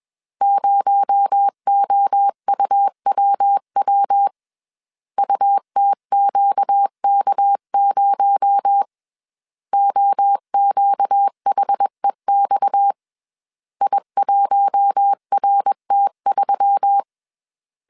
Morse code, heard as a keyed tone, about 21 wpm, 800 Hz. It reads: 0OUWW UTQK0 OQHEX I1RT3
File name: CW / DET / 1 6 1500C S A 20wpm 2048